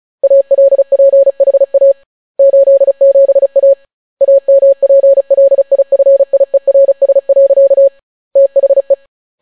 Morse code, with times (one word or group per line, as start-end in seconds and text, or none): 0.2-1.9 ALPHA
2.4-3.7 87A
4.2-7.9 AMPLIFIERS.
8.4-8.9 THE